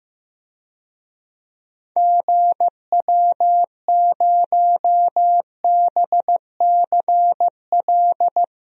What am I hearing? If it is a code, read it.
Morse code, heard as a keyed tone, about 15 wpm, 700 Hz. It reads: GW0BCL
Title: DXCC-100 Call Signs - 6 Characters - 15wpm